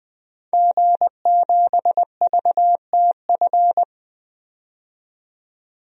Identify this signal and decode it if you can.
Morse code, heard as a keyed tone, about 20 wpm, 700 Hz. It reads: G7VTF
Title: DXCC-100 Call Signs - 5 Characters - 20wpm